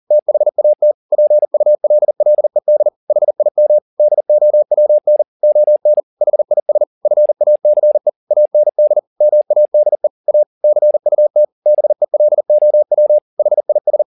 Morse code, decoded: THAT PULLED HIM DOWN ON HIS FACE AND MADE A CUT BELOW HIS

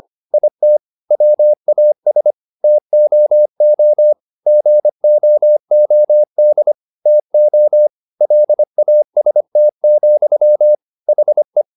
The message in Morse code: IT WAS TOO GOOD TO LAST, HE

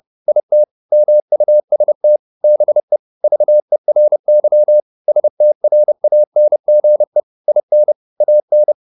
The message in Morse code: IT MUST BE VERY STRANGE IN AN

